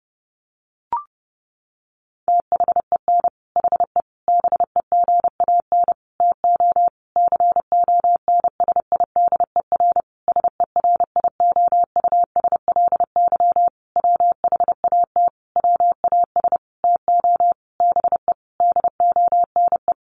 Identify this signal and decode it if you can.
Morse code, heard as a keyed tone, about 30 wpm, 700 Hz. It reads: THEN HE BEGAN TO CONSIDER SERIOUSLY WHAT WAS TO BE DONE
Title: Sentences from Top 1200 Words 30wpm